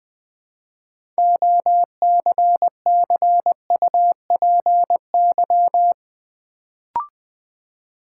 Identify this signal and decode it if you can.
Morse code, heard as a keyed tone, about 20 wpm, 700 Hz. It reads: OCCUPY
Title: Build A Word - 3 Letters At A Time - 6-Letter Words - 20wpm